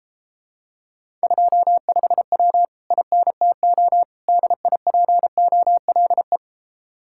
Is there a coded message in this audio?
Yes: Morse code from 1.2 to 6.4 s, heard as a keyed tone, about 34 wpm, 700 Hz. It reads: 25W INTO DIPOLE